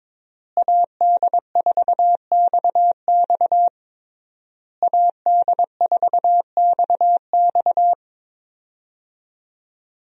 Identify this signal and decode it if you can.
Morse code, heard as a keyed tone, about 22 wpm, 700 Hz. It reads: AD4XX AD4XX
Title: USA Call Signs - 5 Characters - Repeated 3x - 22wpm